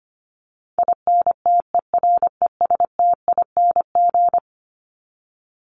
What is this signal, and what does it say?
Morse code, heard as a keyed tone, about 25 wpm, 700 Hz. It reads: INTERESTING
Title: ING Suffix - Top 100 Words with Base Words 25wpm